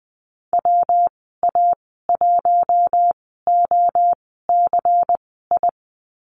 Morse code, heard as a keyed tone, about 20 wpm, spaced as for 14 wpm, 700 Hz. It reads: WA1OCI
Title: 179 Lesson 29g - Slash - Call Signs 20wpm